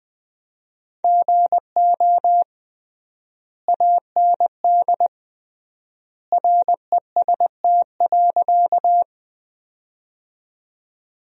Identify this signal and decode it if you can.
Morse code, heard as a keyed tone, about 20 wpm, 700 Hz. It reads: GO AND REST.
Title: 3x Word-Spacing - Sentences from Top 500 Words - 20wpm